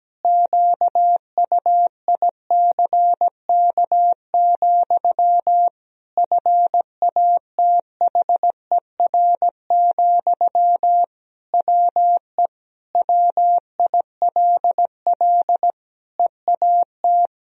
Morse code, heard as a keyed tone, about 17 wpm, 700 Hz. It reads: QUICK, FATHER, WE WILL EAT
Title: Sentences from Top 1000 Words - Review Part 3 - 17wpm